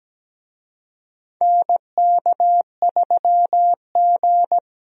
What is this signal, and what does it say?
Morse code, heard as a keyed tone, about 17 wpm, 700 Hz. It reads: NK3G
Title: USA Call Signs - 4 Characters - Repeated 2x - 17wpm